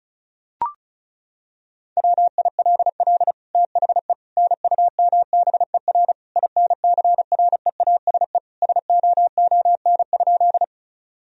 Morse code, heard as a keyed tone, about 36 wpm, 700 Hz. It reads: WILL THE NUMBER INCREASE SOON?
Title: Sentences from Top 1000 Words - Review Part 4 - 35wpm